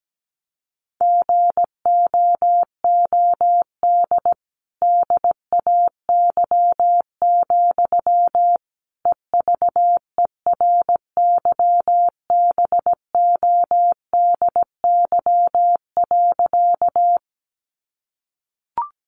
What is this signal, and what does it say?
Morse code, heard as a keyed tone, about 17 wpm, 700 Hz. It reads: GOOD DAY, EVERYBODY.
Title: Sentences from Top 1400 Words 17wpm